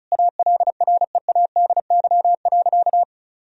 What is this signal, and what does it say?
Morse code, heard as a keyed tone, about 36 wpm, 700 Hz. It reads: ALREADY.
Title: Sentences from Top 1000 Words - Review Part 2 - 35wpm